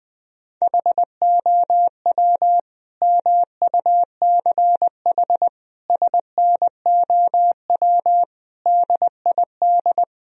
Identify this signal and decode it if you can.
Morse code, heard as a keyed tone, about 20 wpm, 700 Hz. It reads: HOW MUCH SNOW DID
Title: Sentences from Top 1000 Words - Review Part 3 - 20wpm